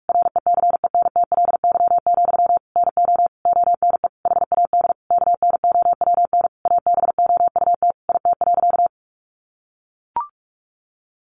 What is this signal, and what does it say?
Morse code, heard as a keyed tone, about 44 wpm, 700 Hz. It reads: RECENTLY, NO ONE HAD KNOWN ABOUT IT.
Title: Sentences from Top 1200 Words 45wpm